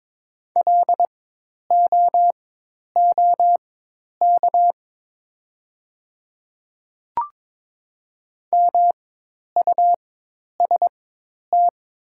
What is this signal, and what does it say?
Morse code, heard as a keyed tone, about 22 wpm, spaced as for 10 wpm, 700 Hz. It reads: LOOK MUST
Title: 4x Character Spacing - Top 300 Words - 22wpm